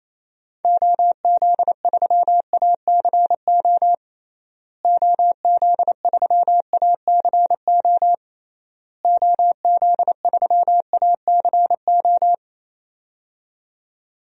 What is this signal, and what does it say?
Morse code, heard as a keyed tone, about 28 wpm, 700 Hz. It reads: OZ3ACO OZ3ACO OZ3ACO
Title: DXCC-100 Call Signs - 6 Characters - Repeated 3x - 28wpm